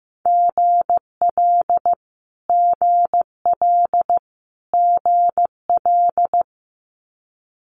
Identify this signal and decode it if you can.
Morse code, heard as a keyed tone, about 15 wpm, 700 Hz. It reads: GL GL GL
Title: Multi-letter Phonemes Repeated 3x 15wpm